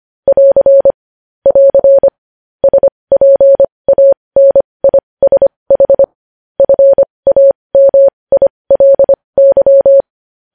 Morse code, heard as a keyed tone, about 25 wpm, spaced as for 20 wpm, 550 Hz. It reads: <AR> <AR> SPANISH FAMILY
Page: News Headlines in Morse Code at 20 WPM